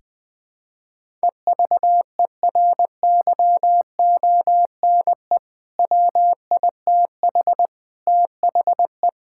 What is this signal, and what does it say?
Morse code, heard as a keyed tone, about 20 wpm, 700 Hz. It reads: EVERYONE WITH THE
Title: Sentences with 4-Word Phrases - 20wpm